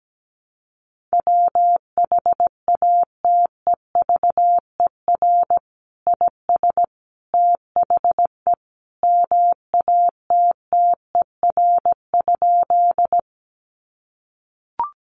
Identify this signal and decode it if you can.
Morse code, heard as a keyed tone, about 17 wpm, 700 Hz. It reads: WHATEVER IS THE MATTER?
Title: Sentences from Top 1100 Words 17wpm